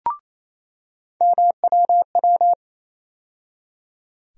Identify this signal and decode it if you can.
Morse code, heard as a keyed tone, about 27 wpm, 700 Hz. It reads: MWW
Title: USA Call Sign Postfixes - 3 Letters 28wpm